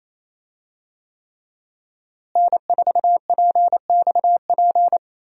Morse code, heard as a keyed tone, about 28 wpm, 700 Hz. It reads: N4PXP
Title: USA Call Signs - 5 Characters - Repeated 3x - 28wpm